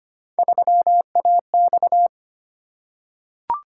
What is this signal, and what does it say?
Morse code, heard as a keyed tone, about 25 wpm, 700 Hz. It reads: 3AX
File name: Three Letter-Number Combinations - 25wpm